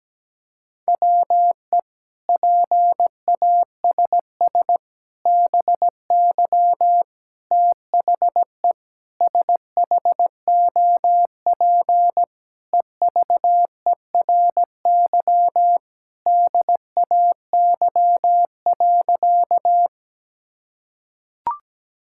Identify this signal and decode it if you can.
Morse code, heard as a keyed tone, about 17 wpm, 700 Hz. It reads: WE PASS BY THE SHOP EVERY DAY.
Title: Sentences from Top 1600 Words 17wpm